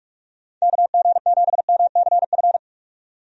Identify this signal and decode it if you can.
Morse code, heard as a keyed tone, about 46 wpm, 700 Hz. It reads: KG7MCF
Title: USA Call Signs - 6 Characters - 45wpm